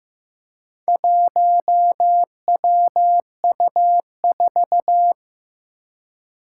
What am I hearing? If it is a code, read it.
Morse code, heard as a keyed tone, about 15 wpm, 700 Hz. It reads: 1WU4
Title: Four Challenging Letter-Number Combinations 15wpm